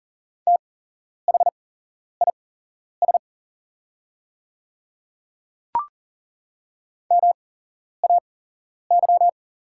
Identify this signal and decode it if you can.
Morse code, heard as a keyed tone, about 40 wpm, spaced as for 11 wpm, 690 Hz. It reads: THIS MAY
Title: 8x Character Spacing - Top 100 Words - 40wpm